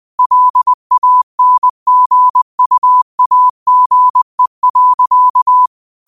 Morse code, heard as a keyed tone, about 20 wpm, 1000 Hz. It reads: LANGUAGE.